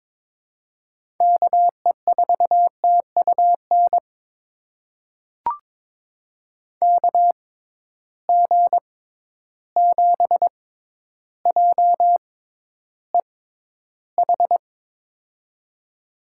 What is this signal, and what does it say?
Morse code, heard as a keyed tone, about 22 wpm, spaced as for 11 wpm, 700 Hz. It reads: KE4TUN KG7JEH
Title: USA Call Signs - 6 Characters - 6x Character Spacing - 22wpm